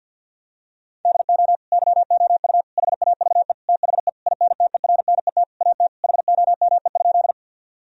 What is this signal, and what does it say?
Morse code, heard as a keyed tone, about 50 wpm, 700 Hz. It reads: DO YOU HAVE THE INTERNET AT HOME?
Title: Sentences from Top 1000 Words - Review Part 4 - 50wpm